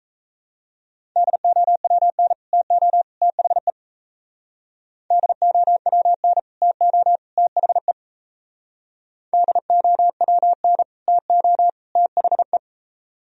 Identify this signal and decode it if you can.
Morse code, heard as a keyed tone, about 37 wpm, 700 Hz. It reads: DOWN TO THE DOWN TO THE DOWN TO THE